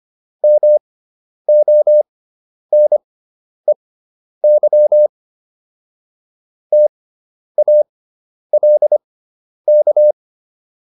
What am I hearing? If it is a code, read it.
Morse code, heard as a keyed tone, about 25 wpm, spaced as for 10 wpm, 600 Hz. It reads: MONEY TALK